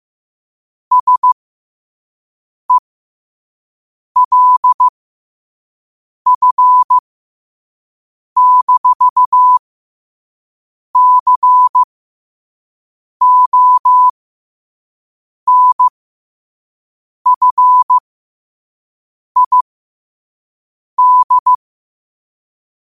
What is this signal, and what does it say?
Morse code, heard as a keyed tone, about 15 wpm, spaced as for 5 wpm, 1000 Hz. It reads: SELF-CONFID
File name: qotdfiles / 5wpm